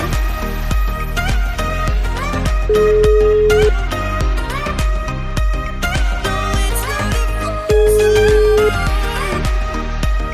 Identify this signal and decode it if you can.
Ringback tone (Europe, 425 Hz), with noise.